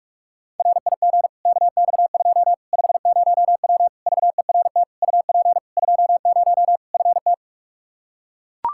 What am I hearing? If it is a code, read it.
Morse code, heard as a keyed tone, about 41 wpm, 700 Hz. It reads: RIG KX2 50W VERT UP 20 FT
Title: Practice QSO - Exchange 2 - Info - 45wpm